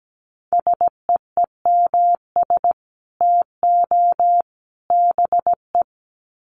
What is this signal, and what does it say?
Morse code, heard as a keyed tone, about 17 wpm, 700 Hz. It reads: SEEMS TO BE
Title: Sets of 3 Words 17wpm